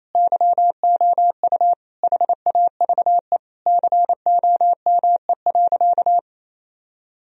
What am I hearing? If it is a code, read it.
Morse code, heard as a keyed tone, about 28 wpm, 700 Hz. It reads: YOU HAVE COME.
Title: Sentences from Top 1000 Words - Review Part 2 - 28wpm